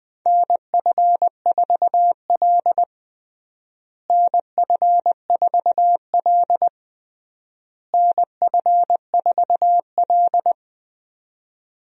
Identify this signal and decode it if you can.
Morse code, heard as a keyed tone, about 20 wpm, 700 Hz. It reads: NF4L NF4L NF4L
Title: USA Call Signs - 4 Characters - Repeated 3x - 20wpm